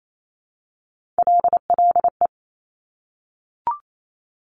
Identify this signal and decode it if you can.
Morse code, heard as a keyed tone, about 28 wpm, 700 Hz. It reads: LLE
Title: Top Three Letter Combinations in Words 28wpm